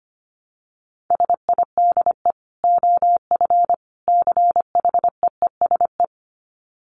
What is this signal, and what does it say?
Morse code, heard as a keyed tone, about 25 wpm, 700 Hz. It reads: SIDE OF CHEESE
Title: Dit Challenge 25wpm